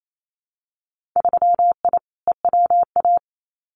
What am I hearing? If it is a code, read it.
Morse code, heard as a keyed tone, about 28 wpm, 700 Hz. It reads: 3I EWA
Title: Winter Field Day Exchanges Encore 28wpm